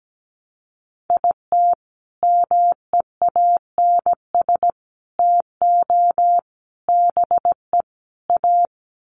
Morse code, heard as a keyed tone, about 17 wpm, 700 Hz. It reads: IT MEANS TO BE A
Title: Sets of 5 Words Encore2 17wpm